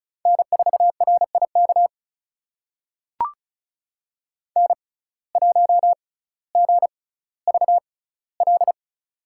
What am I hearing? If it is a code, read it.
Morse code, heard as a keyed tone, about 34 wpm, spaced as for 18 wpm, 700 Hz. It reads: N4RIK N1GVL
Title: USA Call Signs - 5 Characters - 6x Character Spacing - 35wpm